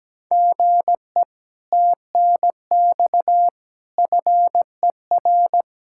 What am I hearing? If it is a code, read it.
Morse code, heard as a keyed tone, about 17 wpm, 700 Hz. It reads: GE TNX FER
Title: Speed-Racing Common QSO Phrases Encore - 17wpm